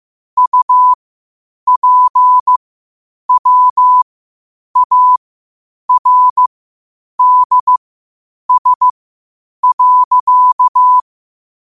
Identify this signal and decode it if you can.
Morse code, heard as a keyed tone, about 15 wpm, spaced as for 8 wpm, 1000 Hz. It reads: UPWARDS.
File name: qotdfiles / 8wpm